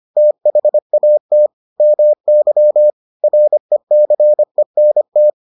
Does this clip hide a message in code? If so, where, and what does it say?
Yes: Morse code from 0.2 to 5.3 s, heard as a keyed tone, about 25 wpm, 600 Hz. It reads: THAT MY RECENT